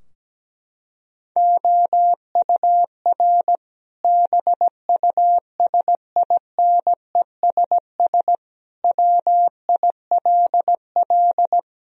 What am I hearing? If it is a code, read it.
Morse code, heard as a keyed tone, about 17 wpm, 700 Hz. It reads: OUR BUSINESS WILL
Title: Sentences from Top 1000 Words - Review Part 4 - 17wpm